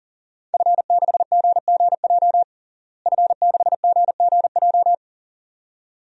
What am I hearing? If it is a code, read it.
Morse code, heard as a keyed tone, about 40 wpm, 700 Hz. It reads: F6GGJ F6GGJ
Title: DXCC-100 Call Signs - 5 Characters - Repeated 2x - 40wpm